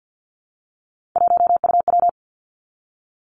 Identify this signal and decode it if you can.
Morse code, heard as a keyed tone, about 50 wpm, 700 Hz. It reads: 1UW
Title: Three Challenging Letter-Number Combinations 50wpm